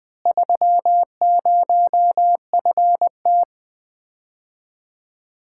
Morse code, heard as a keyed tone, about 20 wpm, 700 Hz. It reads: 30FT
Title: Common QSO Phrases - 4x Word-Spacing - 20wpm